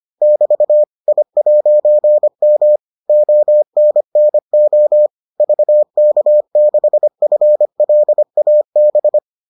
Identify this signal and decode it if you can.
Morse code, heard as a keyed tone, about 25 wpm, 600 Hz. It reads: <BT> I'M ONNO VK6FLAB